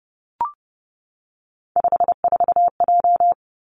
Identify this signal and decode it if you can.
Morse code, heard as a keyed tone, about 30 wpm, 700 Hz. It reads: 54J
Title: Three Challenging Letter-Number Combinations 30wpm